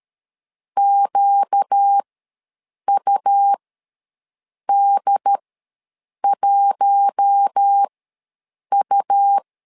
Morse code, heard as a keyed tone, about 13 wpm, spaced as for 7 wpm, 800 Hz. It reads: QUD1U
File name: LEZIONI / 4 Q1 ws 12wpm 2129